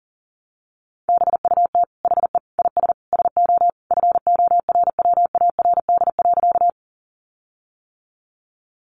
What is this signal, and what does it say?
Morse code, heard as a keyed tone, about 40 wpm, 700 Hz. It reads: BUT HE IS SO FORWARD.
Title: Sentences from Top 1300 Words 40wpm